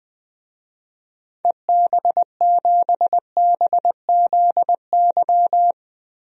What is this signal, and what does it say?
Morse code, heard as a keyed tone, about 20 wpm, 700 Hz. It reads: EB7BZY
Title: DXCC-100 Call Signs - 6 Characters - 20wpm